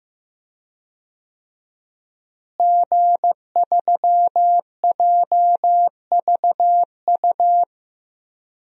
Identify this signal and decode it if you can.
Morse code, heard as a keyed tone, about 15 wpm, 700 Hz. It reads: G3JVU
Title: DXCC-100 Call Signs - 5 Characters - Repeated 2x - 15wpm